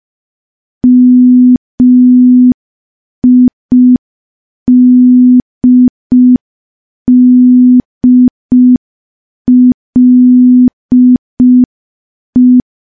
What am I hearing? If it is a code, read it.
Morse code, heard as a keyed tone, about 5 wpm, 250 Hz. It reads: MIDDLE
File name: morse / morse-training-05WPM